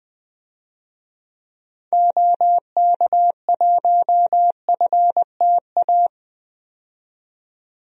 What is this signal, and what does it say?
Morse code, heard as a keyed tone, about 20 wpm, 700 Hz. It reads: OK1FTA
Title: DXCC-100 Call Signs - 6 Characters - 20wpm